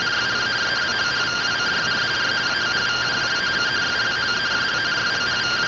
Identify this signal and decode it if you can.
Two-tone teleprinter signal, about 170 Hz shift.